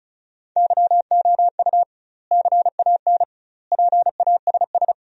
Morse code, heard as a keyed tone, about 36 wpm, 700 Hz. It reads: YOU CAN PASS
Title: 2x Word-Spacing - Sentences from Top 400 Words - 35wpm